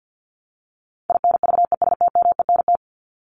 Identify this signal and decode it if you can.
Morse code, heard as a keyed tone, about 50 wpm, 700 Hz. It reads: INVESTMENT
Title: Top 501-750 Words 50wpm